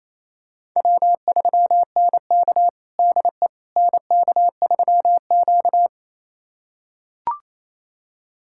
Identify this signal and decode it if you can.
Morse code, heard as a keyed tone, about 28 wpm, 700 Hz. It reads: W3NK DE NK3Q
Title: USA Callsign DE Callsign - 4 Characters - 28wpm